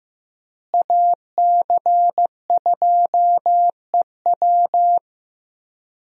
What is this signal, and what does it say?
Morse code, heard as a keyed tone, about 15 wpm, 700 Hz. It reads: AC2EW
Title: USA Call Signs - 5 Characters - Repeated 3x - 15wpm